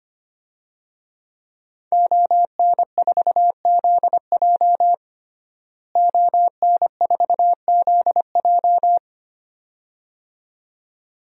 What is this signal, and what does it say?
Morse code, heard as a keyed tone, about 25 wpm, 700 Hz. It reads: ON4ZJ ON4ZJ
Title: DXCC-100 Call Signs - 5 Characters - Repeated 2x - 25wpm